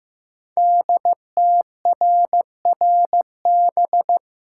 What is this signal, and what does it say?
Morse code, heard as a keyed tone, about 15 wpm, 700 Hz. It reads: DTRRB
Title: Code Groups - 5 Letters - 15wpm